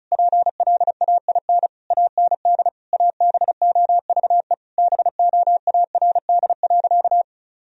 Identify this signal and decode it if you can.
Morse code, heard as a keyed tone, about 36 wpm, 700 Hz. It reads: PLAIN AND ABOVE BOARD.
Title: Sentences from Top 1000 Words - Review Part 4 - 35wpm